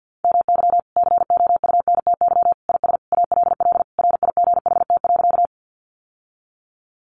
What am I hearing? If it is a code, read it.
Morse code, heard as a keyed tone, about 50 wpm, 700 Hz. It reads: MY COUNTY IS ALL RIGHT.
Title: Sentences from Top 1200 Words 50wpm